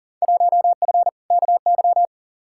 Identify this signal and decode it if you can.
Morse code, heard as a keyed tone, about 38 wpm, 700 Hz. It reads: 1F KY
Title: Field Day Exchanges Encore - 40wpm